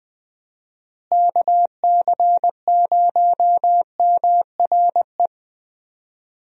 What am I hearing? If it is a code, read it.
Morse code, heard as a keyed tone, about 21 wpm, 700 Hz. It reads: KC0MRE
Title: USA Call Signs - 6 Characters - Repeated 2x - 20wpm